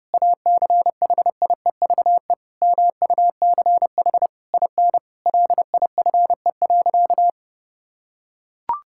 Morse code, heard as a keyed tone, about 29 wpm, 700 Hz. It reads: ACHIEVE MUCH IN LIFE.
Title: Sentences from Top 1000 Words - Review Part 6 - 30wpm